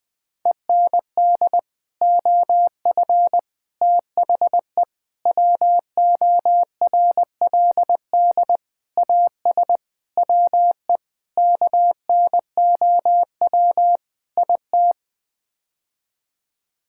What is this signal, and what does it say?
Morse code, heard as a keyed tone, about 21 wpm, 700 Hz. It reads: END OF THE WORLD AS WE KNOW IT
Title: Sets of 8 Words - 20wpm